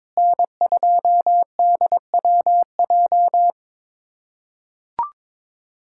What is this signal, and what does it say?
Morse code, heard as a keyed tone, about 22 wpm, 700 Hz. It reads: N2DWJ
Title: USA Callsign DE Callsign - 5 Characters - 22wpm